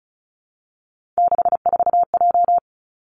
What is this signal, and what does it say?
Morse code, heard as a keyed tone, about 36 wpm, 700 Hz. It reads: 64J
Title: Three Challenging Letter-Number Combinations 35wpm